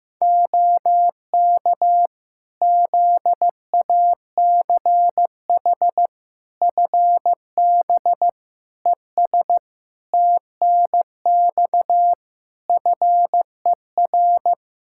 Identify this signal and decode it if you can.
Morse code, heard as a keyed tone, about 15 wpm, 700 Hz. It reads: OK ZACH FB ES TNX FER
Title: Practice QSO - Exchange 2 - Info - 15wpm